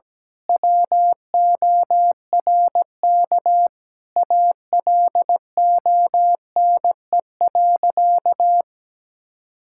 Morse code, heard as a keyed tone, about 17 wpm, 700 Hz. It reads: WORK ALONE.
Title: Sentences from Top 1600 Words 17wpm